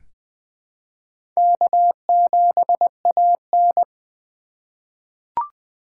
Morse code, heard as a keyed tone, about 20 wpm, 700 Hz. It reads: K7AN
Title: USA Call Signs - 4 Characters - 6x Character Spacing - 20wpm